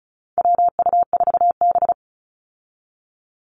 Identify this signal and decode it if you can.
Morse code, heard as a keyed tone, about 34 wpm, 700 Hz. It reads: WU4B
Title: Call Signs Repeated 2x - 35wpm